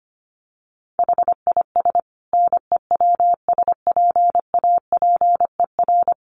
Morse code, heard as a keyed tone, about 25 wpm, 700 Hz. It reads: HIS NEWSPAPER
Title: Sentences from Top 1100 Words 25wpm